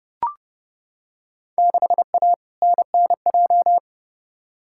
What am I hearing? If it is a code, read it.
Morse code, heard as a keyed tone, about 30 wpm, 700 Hz. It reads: 6A NNJ
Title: Field Day Exchanges Encore - 30wpm